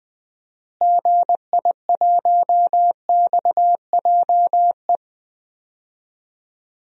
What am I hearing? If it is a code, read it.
Morse code, heard as a keyed tone, about 20 wpm, 700 Hz. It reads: GI1XJE
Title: DXCC-100 Call Signs - 6 Characters - Repeated 2x - 20wpm